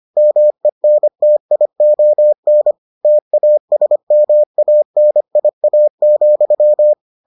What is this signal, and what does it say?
Morse code, heard as a keyed tone, about 25 wpm, 600 Hz. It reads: MENTION TASMANIA,